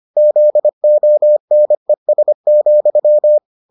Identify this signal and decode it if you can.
Morse code, heard as a keyed tone, about 25 wpm, 600 Hz. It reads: ZONES,